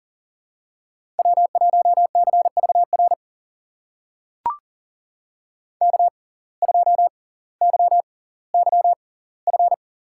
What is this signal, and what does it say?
Morse code, heard as a keyed tone, about 40 wpm, spaced as for 20 wpm, 700 Hz. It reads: W1CVR K2YYF
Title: USA Call Signs - 5 Characters - 6x Character Spacing - 40wpm